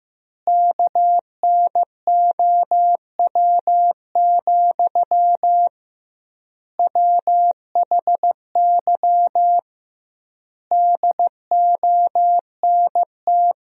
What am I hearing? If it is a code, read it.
Morse code, heard as a keyed tone, about 15 wpm, 700 Hz. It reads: KNOW, WHY DONT
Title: 2x Word-Spacing - Sentences from Top 300 Words - 15wpm